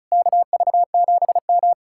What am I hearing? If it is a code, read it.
Morse code, heard as a keyed tone, about 45 wpm, 700 Hz. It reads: KV7M